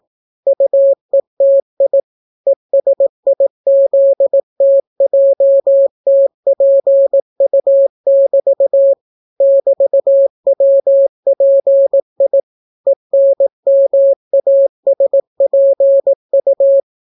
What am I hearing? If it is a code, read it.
Morse code, heard as a keyed tone, about 18 wpm, 550 Hz. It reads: UETI ESIZTJTPU<BT> <BT>WPI ENMASPU